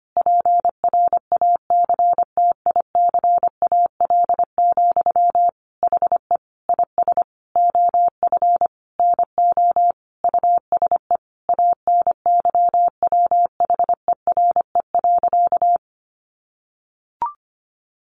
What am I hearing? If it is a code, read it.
Morse code, heard as a keyed tone, about 25 wpm, 700 Hz. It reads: PRACTICAL, HE IS OF NO USE ANYWHERE.
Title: Sentences from Top 1400 Words 25wpm